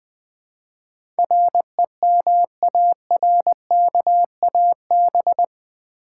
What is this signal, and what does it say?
Morse code, heard as a keyed tone, about 20 wpm, 700 Hz. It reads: REMARKAB
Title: Build A Word - 4 Letters At A Time - 10-Letter Words - 20wpm